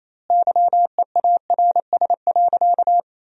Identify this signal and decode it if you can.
Morse code, heard as a keyed tone, about 28 wpm, 700 Hz. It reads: YEARS.
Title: 2x Word-Spacing - Sentences from Top 200 Words - 28wpm